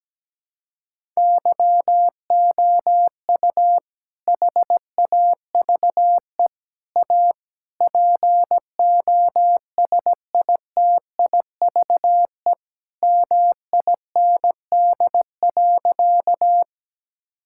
Morse code, heard as a keyed tone, about 17 wpm, 700 Hz. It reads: YOU HAVE A POSITIVE MIND.
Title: Sentences from Top 1000 Words - Review Part 5 - 17wpm